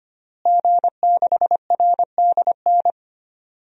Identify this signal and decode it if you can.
Morse code, heard as a keyed tone, about 25 wpm, 700 Hz. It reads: G6RDN
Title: DXCC-100 Call Signs - 5 Characters - Repeated 2x - 25wpm